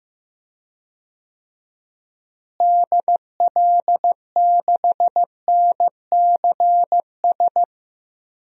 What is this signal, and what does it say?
Morse code, heard as a keyed tone, about 15 wpm, 700 Hz. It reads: DL6NCS
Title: DXCC-100 Call Signs - 6 Characters - Repeated 2x - 15wpm